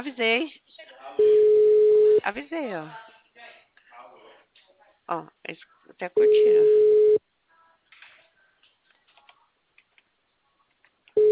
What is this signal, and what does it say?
Ringback tone (Europe, 425 Hz), with speech.
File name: out- -2011- - - .15319